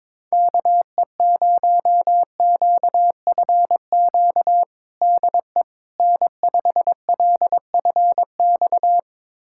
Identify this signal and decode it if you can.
Morse code, heard as a keyed tone, about 22 wpm, 700 Hz. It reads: KE0QFQ DE N5LFX
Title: USA Callsign DE Callsign 22wpm